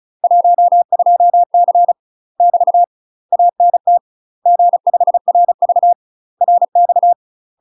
Morse code, heard as a keyed tone, about 36 wpm, 700 Hz. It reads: 12C <BT> ANT G5RV RX
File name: UK MASTER QSOs TRAINING / 35WPM / UK MASTER QSOs 0009